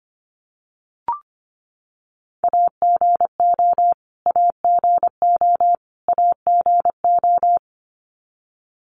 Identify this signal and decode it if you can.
Morse code, heard as a keyed tone, about 25 wpm, 700 Hz. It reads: AGO AGO AGO
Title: Top 500 Words Repeated 25wpm